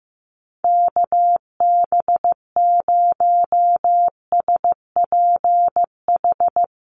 Morse code, heard as a keyed tone, about 15 wpm, 700 Hz. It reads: KB0SPH
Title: USA Call Signs Repeated 2x 15wpm